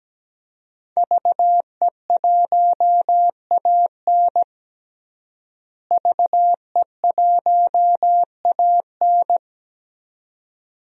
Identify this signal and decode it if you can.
Morse code, heard as a keyed tone, about 17 wpm, 700 Hz. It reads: VE1AN VE1AN
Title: DXCC-100 Call Signs - 5 Characters - Repeated 2x - 17wpm